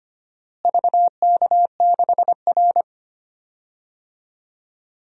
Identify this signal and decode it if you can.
Morse code, heard as a keyed tone, about 25 wpm, 700 Hz. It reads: VK6R
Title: DXCC-100 Call Signs - 4 Characters - Repeated 2x - 25wpm